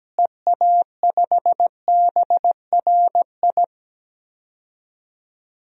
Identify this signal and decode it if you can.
Morse code, heard as a keyed tone, about 17 wpm, 700 Hz. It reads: EA5BRI
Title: DXCC-100 Call Signs - 6 Characters - Repeated 3x - 17wpm